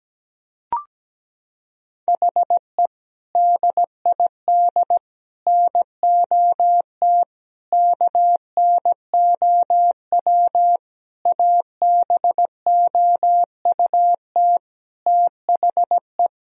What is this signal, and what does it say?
Morse code, heard as a keyed tone, about 17 wpm, 700 Hz. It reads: HE DID NOT KNOW ABOUT THE
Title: Sentences from Top 1000 Words - Review Part 3 - 17wpm